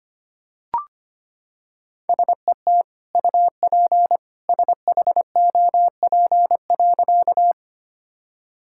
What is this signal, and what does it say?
Morse code, heard as a keyed tone, about 25 wpm, 700 Hz. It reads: SET UP SHOP.
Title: Sentences from Top 1600 Words 25wpm